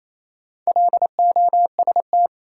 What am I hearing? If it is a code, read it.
Morse code, heard as a keyed tone, about 28 wpm, 700 Hz. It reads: LOST
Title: Sentences from Top 1400 Words - New Word Drill - 28wpm